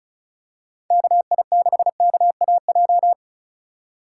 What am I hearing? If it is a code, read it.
Morse code, heard as a keyed tone, about 36 wpm, 700 Hz. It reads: KI6KAJ
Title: USA Call Signs - 6 Characters - 2x Character Spacing - 35wpm